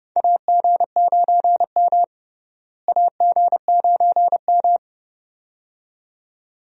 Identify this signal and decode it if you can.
Morse code, heard as a keyed tone, about 31 wpm, 700 Hz. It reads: AG9M AG9M
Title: USA Call Signs - 4 Characters - Repeated 2x - 30wpm